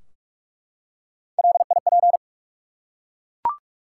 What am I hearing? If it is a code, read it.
Morse code, heard as a keyed tone, about 44 wpm, 700 Hz. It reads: LIP
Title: Three Letter Words - 45wpm